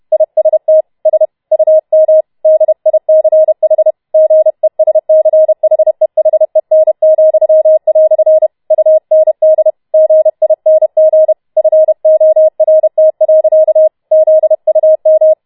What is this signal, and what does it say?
Morse code, heard as a keyed tone, about 31 wpm, 630 Hz. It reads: IST S UM DICH GESCHEHEN," UND GING FORT. ZUM